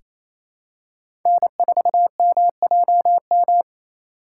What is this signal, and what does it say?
Morse code, heard as a keyed tone, about 28 wpm, 700 Hz. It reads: N4MJM
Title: USA Call Signs - 5 Characters - 4x Character Spacing - 28wpm